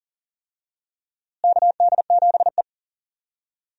Morse code, heard as a keyed tone, about 40 wpm, 700 Hz. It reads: KD7E
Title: USA Call Signs - 4 Characters - Repeated 2x - 40wpm